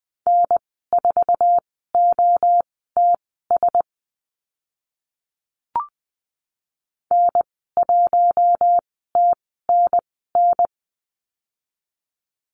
Morse code, heard as a keyed tone, about 20 wpm, spaced as for 14 wpm, 700 Hz. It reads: N4OTS N1TNN
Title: 048 Lesson 9g - Number 4 - Call Signs 20wpm